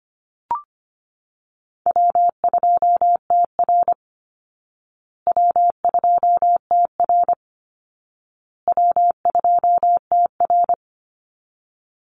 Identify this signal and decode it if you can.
Morse code, heard as a keyed tone, about 25 wpm, 700 Hz. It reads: W2TR W2TR W2TR
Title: Easy Call Signs Repeated 3x - 25wpm